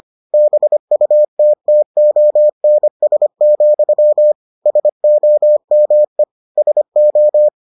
Morse code, heard as a keyed tone, about 25 wpm, 600 Hz. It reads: BUTTONS, SOME SO